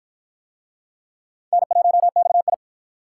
Morse code, heard as a keyed tone, about 50 wpm, 700 Hz. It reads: N1XI